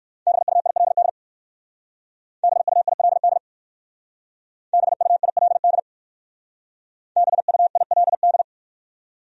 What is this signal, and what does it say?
Morse code, heard as a keyed tone, about 56 wpm, 700 Hz. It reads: BUILD BUILD BUILD BUILD